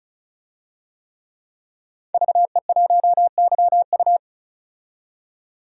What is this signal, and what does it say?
Morse code, heard as a keyed tone, about 35 wpm, 700 Hz. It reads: VE1YU
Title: DXCC-100 Call Signs - 5 Characters - 35wpm